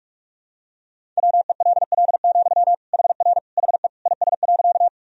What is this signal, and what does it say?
Morse code, heard as a keyed tone, about 41 wpm, 700 Hz. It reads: WELL, HR HE IS.
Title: Sentences from Top 200 Words and Abbreviations 45wpm